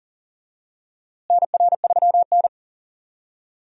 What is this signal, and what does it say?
Morse code, heard as a keyed tone, about 40 wpm, 700 Hz. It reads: NR3N
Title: USA Call Signs - 4 Characters - Repeated 2x - 40wpm